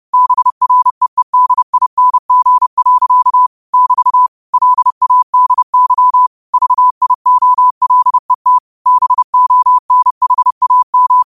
Morse code, heard as a keyed tone, about 30 wpm, 1000 Hz. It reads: BREEDING. <BT> LADY VIOLET BONHAM